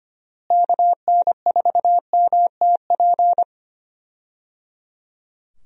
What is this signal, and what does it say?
Morse code, heard as a keyed tone, about 25 wpm, 700 Hz. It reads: KN4MTP
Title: USA Call Signs - 6 Characters - Repeated 3x - 25wpm